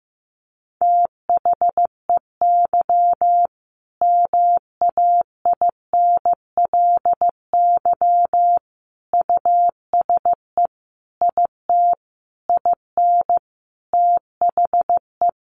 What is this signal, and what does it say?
Morse code, heard as a keyed tone, about 15 wpm, 700 Hz. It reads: THEY MAINLY USE IT IN THE
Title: Sentences from Top 1400 Words 15wpm